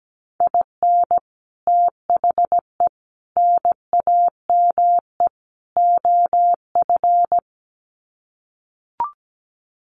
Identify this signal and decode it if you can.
Morse code, heard as a keyed tone, about 17 wpm, 700 Hz. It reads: IN THE NAME OF
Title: Sets of 4 Words 17wpm